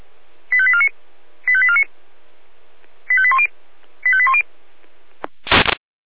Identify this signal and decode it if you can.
Five-tone selective call (ZVEI), heard as quick runs of five tones, tones of about 70 ms. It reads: 85849 85849 85810 85810
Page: /SDR/ZVEI/